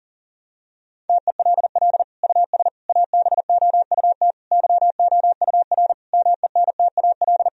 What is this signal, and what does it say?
Morse code, heard as a keyed tone, about 40 wpm, 700 Hz. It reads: TELL US ABOUT YOUR MENTAL
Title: Sentences from Top 1000 Words - Review Part 4 - 40wpm